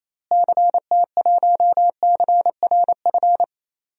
Morse code, heard as a keyed tone, about 28 wpm, 700 Hz. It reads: CT1CRF
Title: DXCC-100 Call Signs - 6 Characters - Repeated 2x - 28wpm